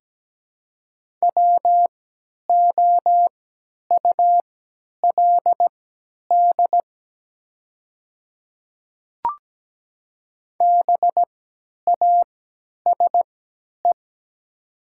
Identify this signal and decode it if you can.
Morse code, heard as a keyed tone, about 17 wpm, spaced as for 10 wpm, 700 Hz. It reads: WOULD BASE
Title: 3x Character Spacing - Top 300 Words - 17wpm